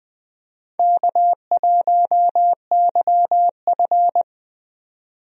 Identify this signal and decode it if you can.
Morse code, heard as a keyed tone, about 20 wpm, 700 Hz. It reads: K1YF
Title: USA Call Signs - 4 Characters - Repeated 3x - 20wpm